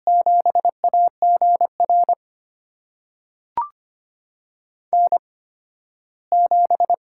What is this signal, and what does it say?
Morse code, heard as a keyed tone, about 25 wpm, spaced as for 15 wpm, 700 Hz. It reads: 7AGR N7
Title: USA Call Signs - 5 Characters - 8x Character Spacing - 25wpm